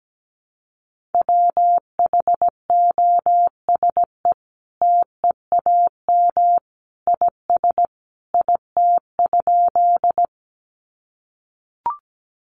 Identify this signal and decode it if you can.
Morse code, heard as a keyed tone, about 17 wpm, 700 Hz. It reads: WHOSE TEAM IS IT?
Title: Sentences from Top 1500 Words 17wpm